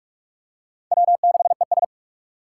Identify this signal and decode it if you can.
Morse code, heard as a keyed tone, about 44 wpm, 700 Hz. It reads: W6ES
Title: Four Challenging Letter-Number Combinations 45wpm